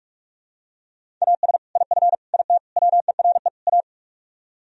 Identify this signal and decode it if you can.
Morse code, heard as a keyed tone, about 47 wpm, 700 Hz. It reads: AS IF IT WERE A